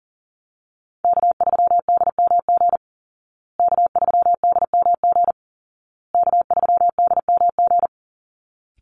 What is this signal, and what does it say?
Morse code, heard as a keyed tone, about 40 wpm, 700 Hz. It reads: K3DMG K3DMG K3DMG
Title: Call Signs Repeated 2x - 40wpm